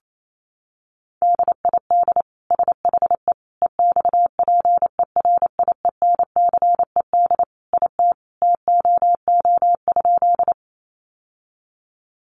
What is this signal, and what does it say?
Morse code, heard as a keyed tone, about 28 wpm, 700 Hz. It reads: DID SHE EXPERIENCED IT TOO?
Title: Sentences from Top 1100 Words 28wpm